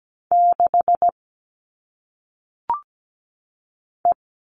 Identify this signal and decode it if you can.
Morse code, heard as a keyed tone, about 17 wpm, 700 Hz. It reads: 6 E
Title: All Cut Numbers - 1-Digit 17wpm